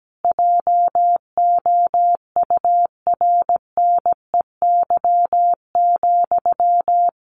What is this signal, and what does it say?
Morse code, heard as a keyed tone, about 17 wpm, 700 Hz. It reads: JOURNEY,
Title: A Sound Of Thunder 17wpm